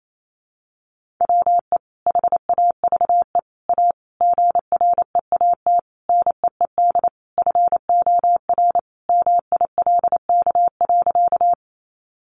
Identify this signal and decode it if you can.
Morse code, heard as a keyed tone, about 28 wpm, 700 Hz. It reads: WE HAVE A GREAT NEED FOR MILK.
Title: Sentences from Top 1500 Words 28wpm